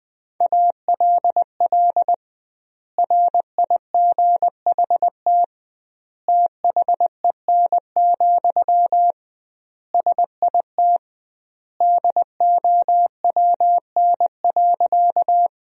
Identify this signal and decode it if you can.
Morse code, heard as a keyed tone, about 21 wpm, 700 Hz. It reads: ALL RIGHT THEN, SIT DOWN.
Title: 2x Word-Spacing - Sentences from Top 500 Words - 20wpm